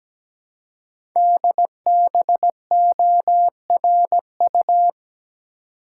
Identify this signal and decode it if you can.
Morse code, heard as a keyed tone, about 17 wpm, 700 Hz. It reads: DBORU
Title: Code Groups - 5 Letters - 17wpm